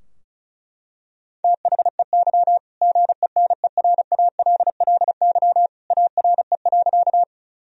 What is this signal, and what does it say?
Morse code, heard as a keyed tone, about 34 wpm, 700 Hz. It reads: THEY GENERALLY ARE.
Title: Sentences from Top 1000 Words - Review Part 4 - 35wpm